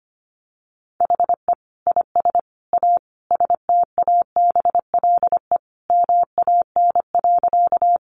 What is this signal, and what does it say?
Morse code, heard as a keyed tone, about 25 wpm, 700 Hz. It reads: HE IS A STABLE MAN.
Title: Sentences from Top 1500 Words 25wpm